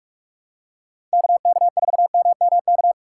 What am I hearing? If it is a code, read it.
Morse code, heard as a keyed tone, about 46 wpm, 690 Hz. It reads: KK4MMK
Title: USA Call Signs - 6 Characters - 2x Character Spacing - 45wpm